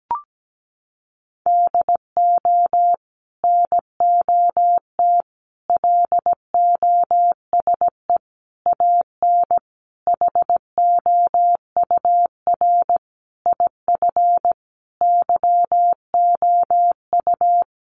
Morse code, heard as a keyed tone, about 17 wpm, 700 Hz. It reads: DO NOT LOSE AN HOUR IF YOU
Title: Sentences from Top 1100 Words 17wpm